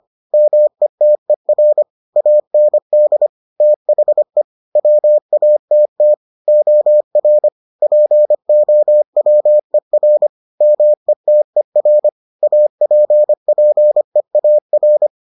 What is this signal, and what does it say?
Morse code, heard as a keyed tone, about 25 wpm, 600 Hz. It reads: METER AND THE WATT OR POWER METER APPEAR